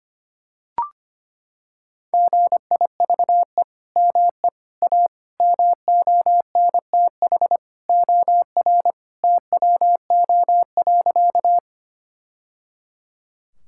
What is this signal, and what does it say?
Morse code, heard as a keyed tone, about 25 wpm, 700 Hz. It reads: GIVE ME A MONTH OR TWO.
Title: Sentences from Top 1000 Words - Review Part 5 - 25wpm